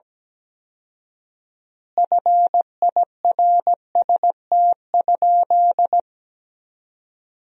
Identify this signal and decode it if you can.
Morse code, heard as a keyed tone, about 17 wpm, 700 Hz. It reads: FIRST?
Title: 4x Word-Spacing - Sentences from Top 100 Words - 17wpm